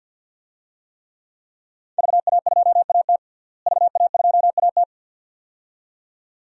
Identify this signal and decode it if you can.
Morse code, heard as a keyed tone, about 53 wpm, 700 Hz. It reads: VA2AT VA2AT
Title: DXCC-100 Call Signs - 5 Characters - Repeated 2x - 50wpm